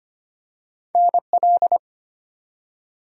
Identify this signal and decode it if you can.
Morse code, heard as a keyed tone, about 25 wpm, 700 Hz. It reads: NL
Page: morse-code-ninja-practice-25wpm